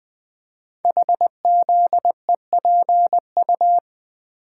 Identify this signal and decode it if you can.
Morse code, heard as a keyed tone, about 20 wpm, 700 Hz. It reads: HZEPU
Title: Code Groups - 5 Letters - 20wpm